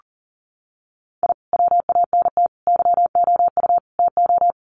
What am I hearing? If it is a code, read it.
Morse code, heard as a keyed tone, about 40 wpm, 700 Hz. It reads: I WANT YOU TO
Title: Sets of 4 Words 40wpm